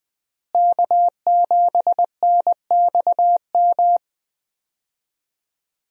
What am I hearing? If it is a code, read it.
Morse code, heard as a keyed tone, about 20 wpm, 700 Hz. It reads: K7NXM
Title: USA Call Signs - 5 Characters - Repeated 2x - 20wpm